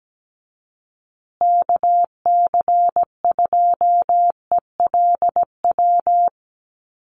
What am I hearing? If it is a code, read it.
Morse code, heard as a keyed tone, about 17 wpm, 700 Hz. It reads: KC2ELW
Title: USA Call Signs Repeated 2x 17wpm